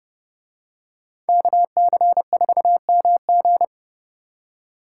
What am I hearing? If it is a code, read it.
Morse code, heard as a keyed tone, about 30 wpm, 700 Hz. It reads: KC4MG